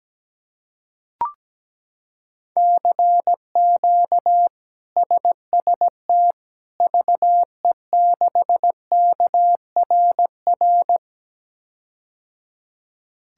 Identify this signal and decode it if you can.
Morse code, heard as a keyed tone, about 17 wpm, 700 Hz. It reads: CQ SST VE6KRR
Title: K1USN SST QSO Practice - 17wpm